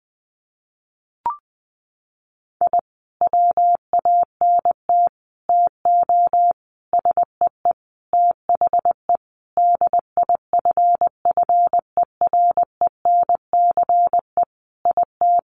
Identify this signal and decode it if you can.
Morse code, heard as a keyed tone, about 20 wpm, 700 Hz. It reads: I WANT TO SEE THE DIFFERENCE IT
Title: Sentences from Top 1300 Words 20wpm